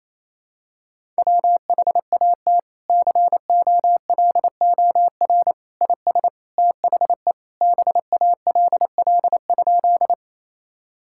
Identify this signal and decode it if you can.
Morse code, heard as a keyed tone, about 28 wpm, 700 Hz. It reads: WHAT COLOR IS THE BALL?
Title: Sentences from Top 1000 Words - Review Part 6 - 28wpm